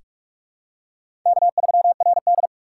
Morse code, heard as a keyed tone, about 43 wpm, 700 Hz. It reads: K3RD
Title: USA Call Signs - 4 Characters - 2x Character Spacing - 45wpm